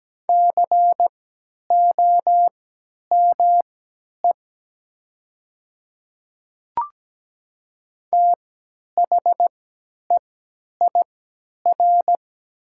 Morse code, heard as a keyed tone, about 17 wpm, spaced as for 10 wpm, 700 Hz. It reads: COME THEIR
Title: 3x Character Spacing - Top 100 Words - 17wpm